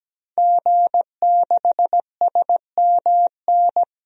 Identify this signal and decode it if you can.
Morse code, heard as a keyed tone, about 17 wpm, 700 Hz. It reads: G6SMN
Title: DXCC-100 Call Signs - 5 Characters - 17wpm